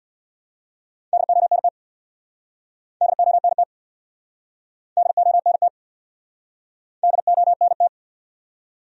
Morse code, heard as a keyed tone, about 68 wpm, 700 Hz. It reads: DONT DONT DONT DONT